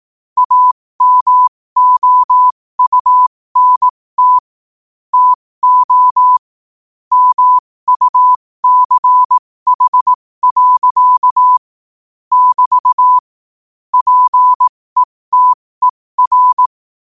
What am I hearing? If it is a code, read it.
Morse code, heard as a keyed tone, about 18 wpm, spaced as for 15 wpm, 1000 Hz. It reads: AMOUNT TO MUCH. <BT> PETER
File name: qotdfiles / 15wpm